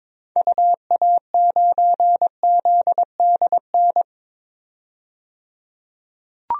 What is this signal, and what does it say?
Morse code, heard as a keyed tone, about 22 wpm, 700 Hz. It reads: UA9ZDN
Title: DXCC-100 Call Signs - 6 Characters - 22wpm